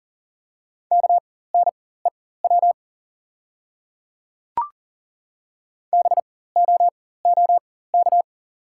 Morse code, heard as a keyed tone, about 38 wpm, spaced as for 18 wpm, 690 Hz. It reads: KNEW BOOK
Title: 4x Character Spacing - Top 500 Words - 40wpm